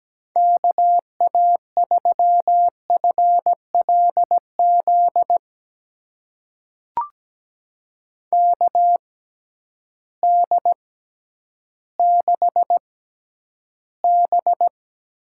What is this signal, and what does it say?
Morse code, heard as a keyed tone, about 17 wpm, spaced as for 10 wpm, 700 Hz. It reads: KA3FLZ KD6B
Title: USA Call Signs - 6 Characters - 6x Character Spacing - 17wpm